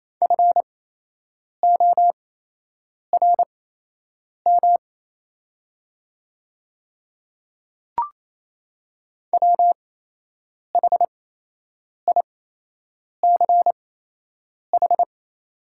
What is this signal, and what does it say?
Morse code, heard as a keyed tone, about 28 wpm, spaced as for 8 wpm, 700 Hz. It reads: FORM WHICH
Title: 8x Character Spacing - Top 200 Words - 28wpm